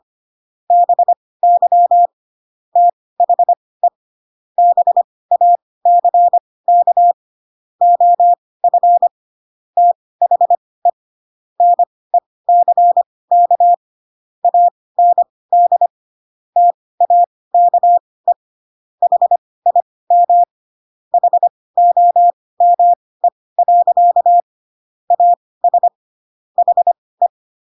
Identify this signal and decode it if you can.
Morse code, heard as a keyed tone, about 25 wpm, spaced as for 18 wpm, 700 Hz. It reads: BY THE BACK OF THE NECK AND TAKE HIM HOME. AS HE
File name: Pinocchio ch3 12 25 ik1hge